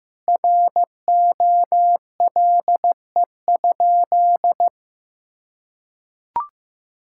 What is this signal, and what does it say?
Morse code, heard as a keyed tone, about 15 wpm, 700 Hz. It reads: ROLE?
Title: Sentences from Top 1000 Words - Review Part 6 - 15wpm